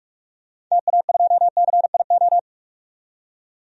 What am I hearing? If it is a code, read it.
Morse code, heard as a keyed tone, about 46 wpm, 700 Hz. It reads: TA2CIO
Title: DXCC-100 Call Signs - 6 Characters - 45wpm